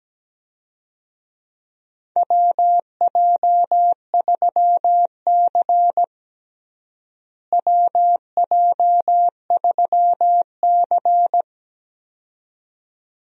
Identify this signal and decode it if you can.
Morse code, heard as a keyed tone, about 17 wpm, 700 Hz. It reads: WJ3C WJ3C
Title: USA Call Signs - 4 Characters - Repeated 2x - 17wpm